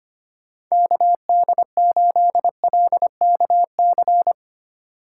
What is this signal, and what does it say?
Morse code, heard as a keyed tone, about 25 wpm, 700 Hz. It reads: KD8LKC
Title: USA Call Signs - 6 Characters - 4x Character Spacing - 25wpm